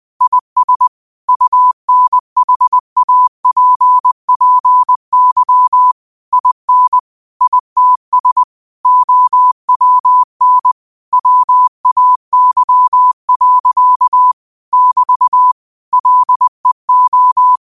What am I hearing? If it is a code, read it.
Morse code, heard as a keyed tone, about 20 wpm, 1000 Hz. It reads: IS UNHAPPY IN ITS OWN WAY. <BT> LEO
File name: qotdfiles / 20wpm